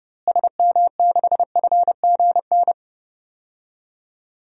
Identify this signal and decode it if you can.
Morse code, heard as a keyed tone, about 30 wpm, 700 Hz. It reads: SM6FGN
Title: DXCC-100 Call Signs - 6 Characters - Repeated 3x - 30wpm